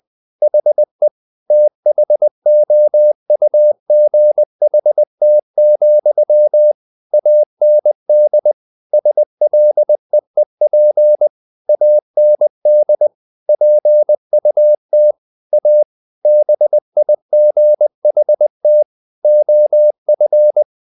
Morse code, heard as a keyed tone, about 20 wpm, 600 Hz. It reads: HE THOUGHT, AND SLEEP AND PUT A BIGHT OF
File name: the old man and the sea / the old man and the sea part 1 20wpm